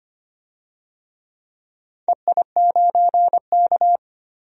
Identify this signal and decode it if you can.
Morse code, heard as a keyed tone, about 25 wpm, 700 Hz. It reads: EI9K
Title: DXCC-100 Call Signs - 4 Characters - 25wpm